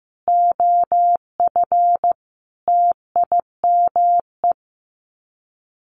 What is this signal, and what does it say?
Morse code, heard as a keyed tone, about 15 wpm, 700 Hz. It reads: OF TIME
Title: Sets of 5 Words Encore2 15wpm